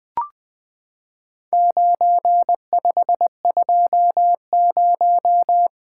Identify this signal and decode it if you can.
Morse code, heard as a keyed tone, about 20 wpm, 700 Hz. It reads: 9520
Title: Four Letter-Number Combinations - 20wpm